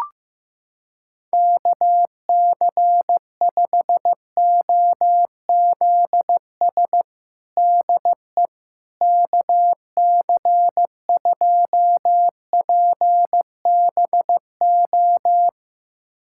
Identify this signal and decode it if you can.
Morse code, heard as a keyed tone, about 15 wpm, 700 Hz. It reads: KC5OZS DE KC2PBO
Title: USA Callsign DE Callsign 15wpm